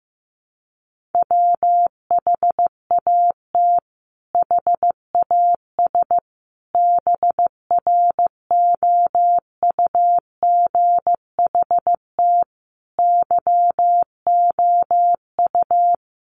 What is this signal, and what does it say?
Morse code, heard as a keyed tone, about 15 wpm, 700 Hz. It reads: WHAT HAS BROUGHT YOU
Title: Sentences from Top 1500 Words 15wpm